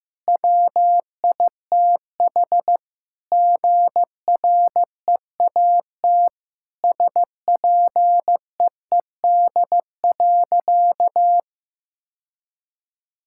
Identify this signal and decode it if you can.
Morse code, heard as a keyed tone, about 16 wpm, 700 Hz. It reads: WITH GREAT SPEED.
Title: Sentences from Top 1000 Words - Review Part 5 - 15wpm